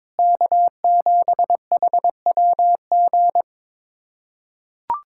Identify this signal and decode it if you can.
Morse code, heard as a keyed tone, about 22 wpm, 700 Hz. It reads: K7HWG
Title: USA Callsign DE Callsign - 5 Characters - 22wpm